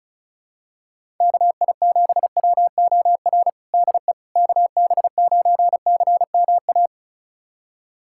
Morse code, heard as a keyed tone, about 36 wpm, 700 Hz. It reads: KI7WOR DE KB9CMA
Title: USA Callsign DE Callsign - 6 Characters - 35wpm